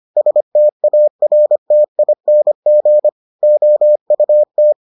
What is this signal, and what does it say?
Morse code, heard as a keyed tone, about 25 wpm, 600 Hz. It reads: STARTING OUT